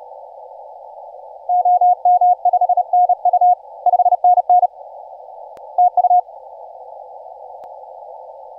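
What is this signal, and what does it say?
Morse code, heard as a keyed tone, about 34 wpm, 720 Hz. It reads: OM5NU 5NN TU
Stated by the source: Morse from DX station 5T5OK on 20 m CW, posted by IZ8GNR.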